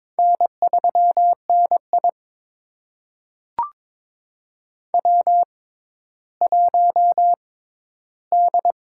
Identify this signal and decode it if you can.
Morse code, heard as a keyed tone, about 22 wpm, spaced as for 13 wpm, 700 Hz. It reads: N3NI W1D
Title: USA Call Signs - 6x Character Spacing - 22wpm